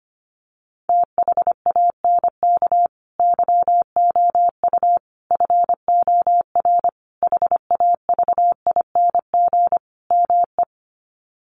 Morse code, heard as a keyed tone, about 25 wpm, 700 Hz. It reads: THANK YOU FOR HAVING ME
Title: Sets of 5 Words Encore 25wpm